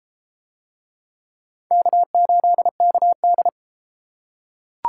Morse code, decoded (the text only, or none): K8KD